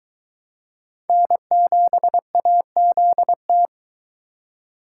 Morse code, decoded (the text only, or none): N7AZT